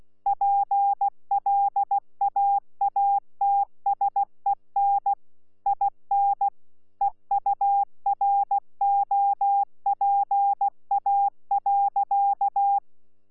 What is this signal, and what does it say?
Morse code, heard as a keyed tone, about 16 wpm, 800 Hz. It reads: PLAATSEN IN EUROPA.
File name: Z PI4VRZ / PI4VRZ 14-jun-2025